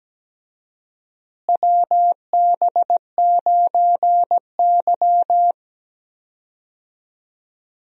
Morse code, heard as a keyed tone, about 17 wpm, 700 Hz. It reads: WB9Y
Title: USA Call Signs - 4 Characters - Repeated 3x - 17wpm